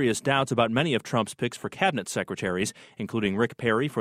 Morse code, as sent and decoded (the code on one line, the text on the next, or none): none